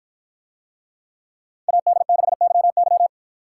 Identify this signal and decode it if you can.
Morse code, heard as a keyed tone, about 53 wpm, 700 Hz. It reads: AD6YY